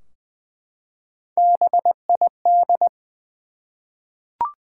Morse code, heard as a keyed tone, about 20 wpm, 700 Hz. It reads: BID
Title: Three Letter Words - 20wpm